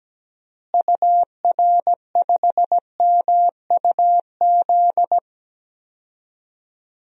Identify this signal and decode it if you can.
Morse code, heard as a keyed tone, about 17 wpm, 700 Hz. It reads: UR5MUZ
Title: DXCC-100 Call Signs - 6 Characters - Repeated 2x - 17wpm